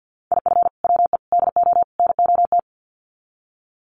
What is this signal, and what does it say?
Morse code, heard as a keyed tone, about 50 wpm, 700 Hz. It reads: IF WE DO NOT
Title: Sets of 4 Words 50wpm